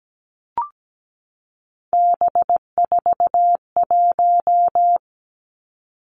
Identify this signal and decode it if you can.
Morse code, heard as a keyed tone, about 17 wpm, 700 Hz. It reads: B41
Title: Three Challenging Letter-Number Combinations 17wpm